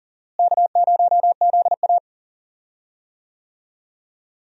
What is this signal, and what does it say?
Morse code, heard as a keyed tone, about 40 wpm, 700 Hz. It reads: K0ZA
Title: USA Call Signs - 4 Characters - Repeated 2x - 40wpm